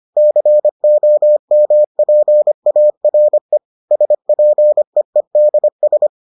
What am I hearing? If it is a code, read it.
Morse code, heard as a keyed tone, about 25 wpm, 600 Hz. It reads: COMPARE SPEEDS